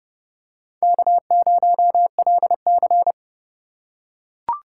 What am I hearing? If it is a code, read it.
Morse code, heard as a keyed tone, about 25 wpm, 700 Hz. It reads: K0LC